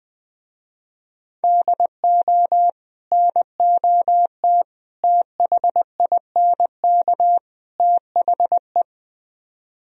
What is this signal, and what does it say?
Morse code, heard as a keyed tone, about 20 wpm, 700 Hz. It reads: DO NOT THINK THE
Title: Speed-Racing - Sets of 4 Words 17wpm